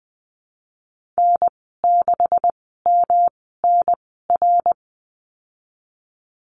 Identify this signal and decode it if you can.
Morse code, heard as a keyed tone, about 20 wpm, spaced as for 14 wpm, 700 Hz. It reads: N6MNR
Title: 173 Lesson 28g - Number 9 - Call Signs 20wpm